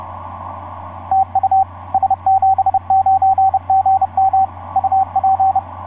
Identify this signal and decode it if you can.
Morse code, heard as a keyed tone, about 30 wpm, 770 Hz, with radio noise. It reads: TU S79GM UP
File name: s79gm f6eyb etc 12m cw 24mar2010 1207